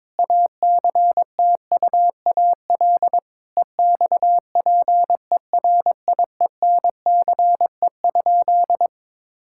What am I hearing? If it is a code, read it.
Morse code, heard as a keyed tone, about 21 wpm, 700 Hz. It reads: ACTUAL EXPERIENCE?
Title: Sentences from Top 1000 Words - Review Part 5 - 22wpm